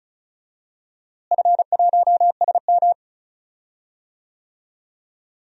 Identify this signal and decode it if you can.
Morse code, heard as a keyed tone, about 36 wpm, 700 Hz. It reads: F1SM
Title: DXCC-100 Call Signs - 4 Characters - Repeated 2x - 35wpm